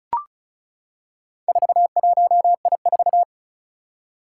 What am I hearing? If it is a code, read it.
Morse code, heard as a keyed tone, about 36 wpm, 700 Hz. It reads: 41I4
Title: Four Challenging Letter-Number Combinations 35wpm